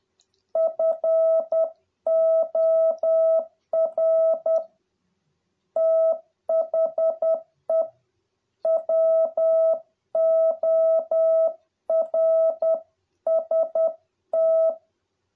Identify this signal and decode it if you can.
Morse code, heard as a keyed tone, about 10 wpm, 650 Hz. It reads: FOR THE WORST